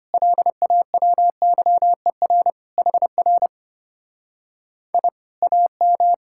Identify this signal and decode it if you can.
Morse code, heard as a keyed tone, about 33 wpm, 700 Hz. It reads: LAWYER HR I AM